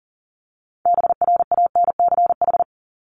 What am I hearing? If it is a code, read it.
Morse code, heard as a keyed tone, about 40 wpm, 700 Hz. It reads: BRANCH
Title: Top 1751-2000 Words 40wpm